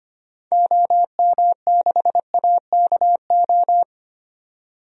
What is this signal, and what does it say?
Morse code, heard as a keyed tone, about 25 wpm, 700 Hz. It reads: OM6AKO
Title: DXCC-100 Call Signs - 6 Characters - 25wpm